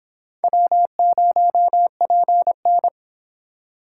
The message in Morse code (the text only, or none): W0PN